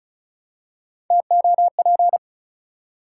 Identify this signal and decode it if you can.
Morse code, heard as a keyed tone, about 36 wpm, 700 Hz. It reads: TOP
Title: Three Letter Words - 35wpm